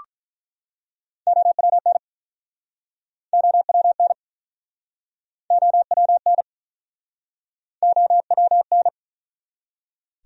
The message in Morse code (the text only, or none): OWN OWN OWN OWN